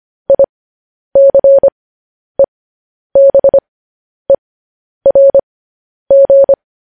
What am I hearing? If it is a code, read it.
Morse code, heard as a keyed tone, about 25 wpm, spaced as for 10 wpm, 550 Hz. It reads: ICEBERG
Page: news headlines in morse code at 10 wpm